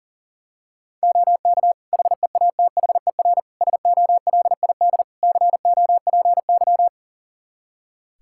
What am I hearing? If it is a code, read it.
Morse code, heard as a keyed tone, about 37 wpm, 700 Hz. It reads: OK HEATHER SOLID COPY
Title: K1USN SST QSO Practice 40wpm